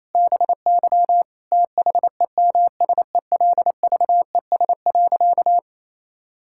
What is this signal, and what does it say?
Morse code, heard as a keyed tone, about 28 wpm, 700 Hz. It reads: BY THEMSELVES.
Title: Sentences from Top 1000 Words - Review Part 5 - 28wpm